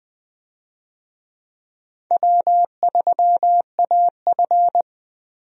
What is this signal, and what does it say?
Morse code, heard as a keyed tone, about 20 wpm, 700 Hz. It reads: W3AF
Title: USA Call Signs - 4 Characters - Repeated 2x - 20wpm